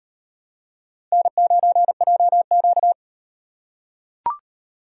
Morse code, heard as a keyed tone, about 38 wpm, 700 Hz. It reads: N9JQ